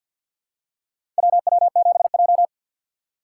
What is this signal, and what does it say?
Morse code, heard as a keyed tone, about 51 wpm, 690 Hz. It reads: WW7J
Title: USA Call Signs - 2x Character Spacing - 50wpm